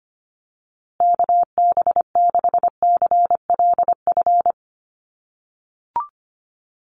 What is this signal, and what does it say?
Morse code, heard as a keyed tone, about 25 wpm, 700 Hz. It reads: KB6CLF
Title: USA Call Signs Repeated 3x 25wpm